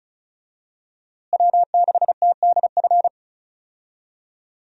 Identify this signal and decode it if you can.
Morse code, heard as a keyed tone, about 36 wpm, 700 Hz. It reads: W6TDF
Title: USA Call Signs - 5 Characters - Repeated 3x - 35wpm